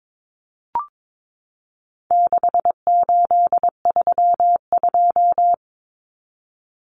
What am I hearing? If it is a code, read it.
Morse code, heard as a keyed tone, about 22 wpm, 700 Hz. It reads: 6832
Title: Numbers - 4-Digit 22wpm